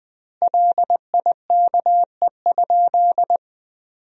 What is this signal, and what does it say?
Morse code, heard as a keyed tone, about 20 wpm, 700 Hz. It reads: LIKE?
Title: 2x Word-Spacing - Sentences from Top 100 Words - 20wpm